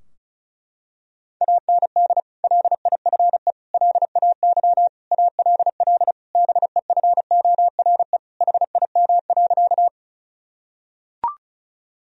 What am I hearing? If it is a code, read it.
Morse code, heard as a keyed tone, about 34 wpm, 700 Hz. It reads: AND LIFE LAY ALL BEFORE HIM.
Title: Sentences from Top 1000 Words - Review Part 2 - 35wpm